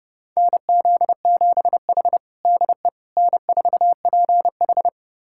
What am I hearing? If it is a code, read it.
Morse code, heard as a keyed tone, about 32 wpm, 700 Hz. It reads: NZ7H DE N4PH
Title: USA Callsign DE Callsign - 4 Characters - 30wpm